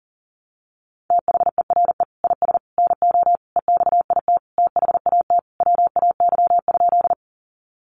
Morse code, heard as a keyed tone, about 40 wpm, 700 Hz. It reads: THERE IS NO EXIT THAT WAY?
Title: Sentences from Top 1100 Words 40wpm